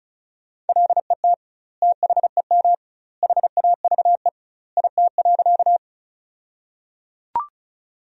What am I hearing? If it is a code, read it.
Morse code, heard as a keyed tone, about 36 wpm, 700 Hz. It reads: LET THEM HAVE IT.
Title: 2x Word-Spacing - Sentences from Top 300 Words - 35wpm